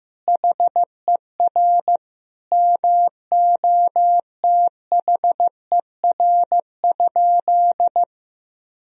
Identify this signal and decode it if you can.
Morse code, heard as a keyed tone, about 15 wpm, 700 Hz. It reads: HER MOTHER?
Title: Sentences from Top 1000 Words - Review Part 3 - 15wpm